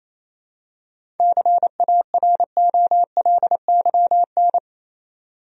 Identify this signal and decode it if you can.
Morse code, heard as a keyed tone, about 28 wpm, 700 Hz. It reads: CAROLYN
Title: USA Common Names - 28wpm